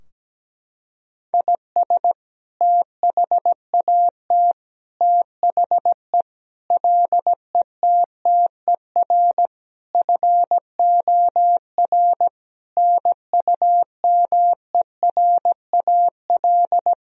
Morse code, heard as a keyed tone, about 17 wpm, 700 Hz. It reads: IS THAT THE LETTER FOR NUMERAL
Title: Sentences from Top 1000 Words - Review Part 2 - 17wpm